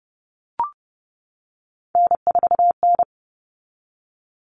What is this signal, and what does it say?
Morse code, heard as a keyed tone, about 30 wpm, 700 Hz. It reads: N4N
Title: Common Cut Numbers - 3-Digit 30wpm